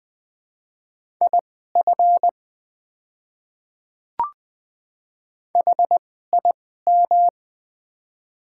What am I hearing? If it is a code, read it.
Morse code, heard as a keyed tone, about 20 wpm, spaced as for 14 wpm, 700 Hz. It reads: IF HIM
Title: 2x Character Spacing - Top 100 Words - 20wpm